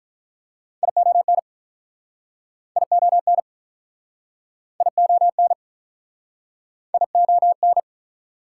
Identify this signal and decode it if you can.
Morse code, heard as a keyed tone, about 45 wpm, 700 Hz. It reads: ION ION ION ION